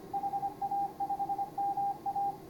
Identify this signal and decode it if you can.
Morse code, heard as a keyed tone, about 24 wpm, 810 Hz, with radio noise. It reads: HA5MA